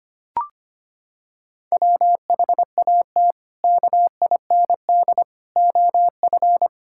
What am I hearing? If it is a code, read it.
Morse code, heard as a keyed tone, about 25 wpm, 700 Hz. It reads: WHAT KIND OF
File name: practice / 25wpm / Sentences with 3-Word Phrases - 25wpm